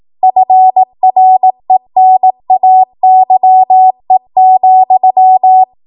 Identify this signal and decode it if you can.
Morse code, heard as a keyed tone, about 18 wpm, 750 Hz. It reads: FRENAYE,